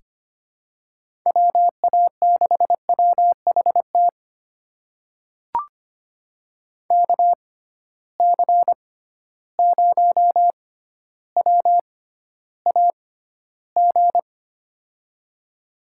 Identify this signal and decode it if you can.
Morse code, heard as a keyed tone, about 25 wpm, spaced as for 13 wpm, 700 Hz. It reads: WA6WHT KC0WAG
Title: USA Call Signs - 6 Characters - 6x Character Spacing - 25wpm